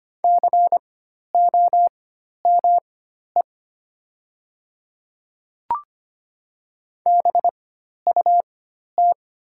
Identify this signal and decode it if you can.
Morse code, heard as a keyed tone, about 25 wpm, spaced as for 12 wpm, 700 Hz. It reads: COME BUT
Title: 4x Character Spacing - Top 100 Words - 25wpm